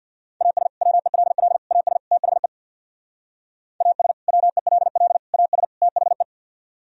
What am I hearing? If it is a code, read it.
Morse code, heard as a keyed tone, about 62 wpm, 700 Hz. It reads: AS WELL AS THE AS WELL AS THE
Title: Speed-Racing - Sets of 4 Words 50wpm